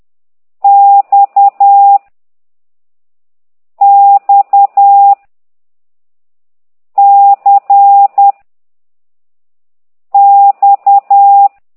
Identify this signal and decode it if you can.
Morse code, heard as a keyed tone, about 10 wpm, 800 Hz. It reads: X X C X
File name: ct2idw / exerc4